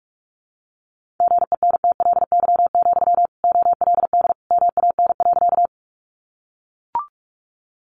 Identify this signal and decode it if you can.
Morse code, heard as a keyed tone, about 44 wpm, 700 Hz. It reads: GENTLY, OLD MAN.
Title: Sentences from Top 1500 Words 45wpm